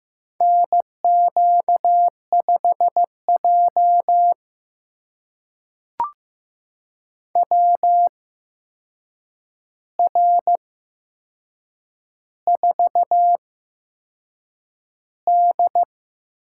Morse code, heard as a keyed tone, about 15 wpm, spaced as for 6 wpm, 700 Hz. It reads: NQ5J WR4D
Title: USA Call Signs - 4 Characters - 8x Character Spacing - 15wpm